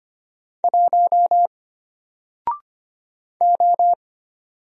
Morse code, heard as a keyed tone, about 25 wpm, 700 Hz. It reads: 1 O
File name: Single-Letter-Number-25wpm